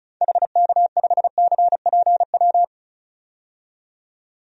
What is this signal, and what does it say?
Morse code, heard as a keyed tone, about 36 wpm, 700 Hz. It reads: HK5CPW
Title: DXCC-100 Call Signs - 6 Characters - 35wpm